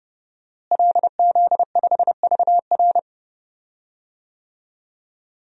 Morse code, heard as a keyed tone, about 30 wpm, 700 Hz. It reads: LZ5VR
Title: DXCC-100 Call Signs - 5 Characters - Repeated 2x - 30wpm